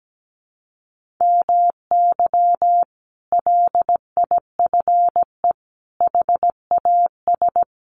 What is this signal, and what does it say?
Morse code, heard as a keyed tone, about 17 wpm, 700 Hz. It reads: MY LIFE HAS
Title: Sentences from Top 1100 Words 17wpm